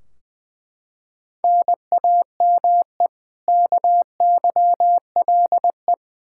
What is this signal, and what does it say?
Morse code, heard as a keyed tone, about 20 wpm, 700 Hz. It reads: NAME KYLE
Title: Practice QSO - Exchange 1 - Beginning - 20wpm